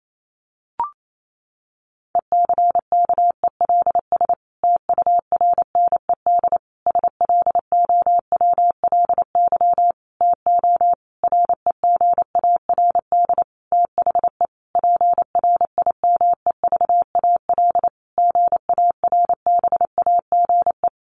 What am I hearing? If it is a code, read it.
Morse code, heard as a keyed tone, about 28 wpm, 700 Hz. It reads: ECKELS TURNED SLOWLY TO REGARD THE PRIMEVAL GARBAGE
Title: A Sound Of Thunder 28wpm